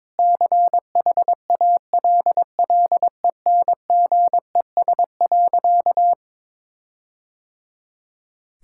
Morse code, heard as a keyed tone, about 22 wpm, 700 Hz. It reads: CHALLENGES.
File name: practice / 22wpm / Sentences with 4-Word Phrases - 22wpm